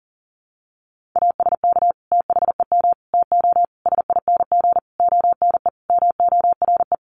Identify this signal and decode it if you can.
Morse code, heard as a keyed tone, about 40 wpm, 700 Hz. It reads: ASK THEM TO SING ONE MORE
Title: Sentences from Top 400 Words 40wpm